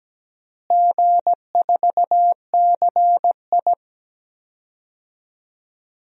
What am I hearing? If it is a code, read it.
Morse code, heard as a keyed tone, about 17 wpm, 700 Hz. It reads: G4CI
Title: DXCC-100 Call Signs - 4 Characters - Repeated 2x - 17wpm